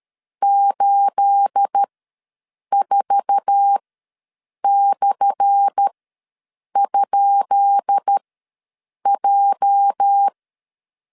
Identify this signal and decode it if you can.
Morse code, heard as a keyed tone, about 13 wpm, spaced as for 7 wpm, 800 Hz. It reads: 84/?J
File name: LEZIONI / 10 SIGNS / 10B INT ws 12wpm 3527